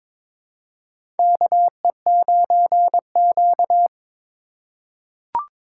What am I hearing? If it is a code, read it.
Morse code, heard as a keyed tone, about 21 wpm, 700 Hz. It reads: KE9Q
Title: USA Call Signs - 4 Characters - 3x Character Spacing - 22wpm